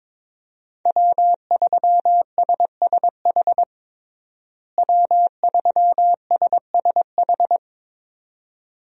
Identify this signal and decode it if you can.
Morse code, heard as a keyed tone, about 22 wpm, 700 Hz. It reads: W3SSH W3SSH
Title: USA Call Signs - 5 Characters - Repeated 2x - 22wpm